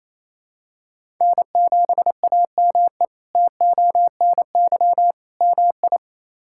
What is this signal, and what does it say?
Morse code, heard as a keyed tone, about 28 wpm, 700 Hz. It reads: N7AME TONY MI
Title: K1USN SST QSO Practice - 28wpm